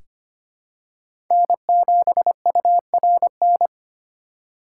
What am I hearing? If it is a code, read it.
Morse code, heard as a keyed tone, about 25 wpm, 700 Hz. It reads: N7URN
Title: USA Call Signs - 5 Characters - 3x Character Spacing - 25wpm